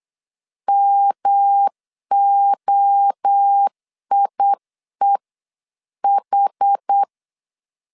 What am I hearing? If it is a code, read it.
Morse code, heard as a keyed tone, about 8 wpm, 800 Hz. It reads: MOIEH